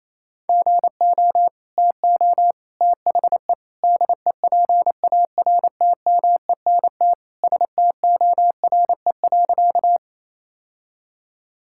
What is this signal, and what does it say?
Morse code, heard as a keyed tone, about 28 wpm, 700 Hz. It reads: GO TO THE DEPARTMENT STORE.
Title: Sentences from Top 1000 Words - Review Part 5 - 28wpm